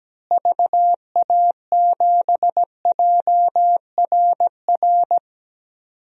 Morse code, heard as a keyed tone, about 17 wpm, 700 Hz. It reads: VA7JRR
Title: DXCC-100 Call Signs - 6 Characters - Repeated 3x - 17wpm